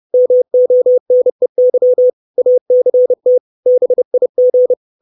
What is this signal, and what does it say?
Morse code, heard as a keyed tone, about 30 wpm, 500 Hz. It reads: MONEY ACT BIG
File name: 632 words / 632 Words 30WPM